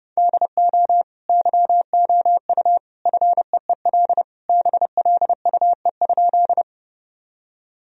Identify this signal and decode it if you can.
Morse code, heard as a keyed tone, about 29 wpm, 700 Hz. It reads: DO YOU FEEL BLUE?
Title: Sentences from Top 1000 Words - Review Part 3 - 30wpm